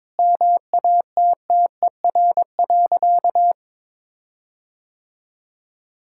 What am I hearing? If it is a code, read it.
Morse code, heard as a keyed tone, about 22 wpm, 700 Hz. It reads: MATTER.
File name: Sentences with 7-Word Phrases - 22wpm